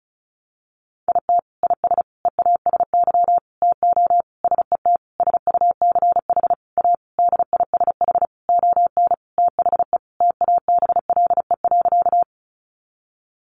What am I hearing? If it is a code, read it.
Morse code, heard as a keyed tone, about 34 wpm, 700 Hz. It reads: IT IS EASY TO SET SUCH A DISH ON THE TABLE.
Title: Sentences from Top 1300 Words 35wpm